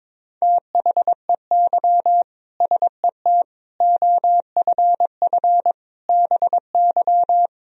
Morse code, heard as a keyed tone, about 21 wpm, 700 Hz. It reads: THEY SET OFF BY
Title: Sentences from Top 1000 Words - Review Part 3 - 22wpm